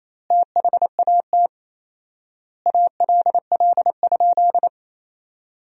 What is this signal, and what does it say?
Morse code, heard as a keyed tone, about 28 wpm, 700 Hz. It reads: THAT ALL?
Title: 4x Word-Spacing - Sentences from Top 200 Words - 28wpm